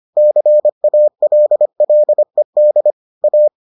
Morse code, heard as a keyed tone, about 25 wpm, 600 Hz. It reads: CALLED A